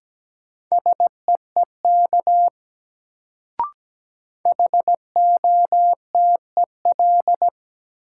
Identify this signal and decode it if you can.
Morse code, heard as a keyed tone, about 17 wpm, 700 Hz. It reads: SEEK HOTEL
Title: Sentences from Top 1600 Words - New Word Drill - 17wpm